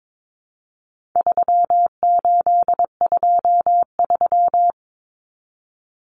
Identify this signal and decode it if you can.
Morse code, heard as a keyed tone, about 22 wpm, 700 Hz. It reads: 3823
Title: Common Cut Numbers - 4-Digit 22wpm